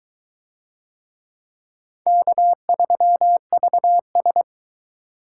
Morse code, heard as a keyed tone, about 22 wpm, 700 Hz. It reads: K3VS